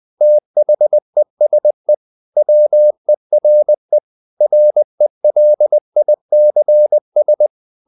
Morse code, heard as a keyed tone, about 20 wpm, 600 Hz. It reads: THESE WERE RELICS
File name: the old man and the sea / the old man and the sea part 1 20wpm